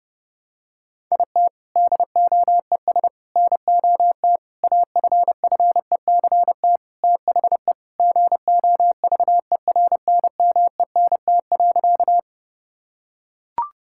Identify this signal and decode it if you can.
Morse code, heard as a keyed tone, about 30 wpm, 700 Hz. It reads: IT DOES NOT AFFECT THE GOVERNMENT.
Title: Sentences from Top 1000 Words - Review Part 6 - 30wpm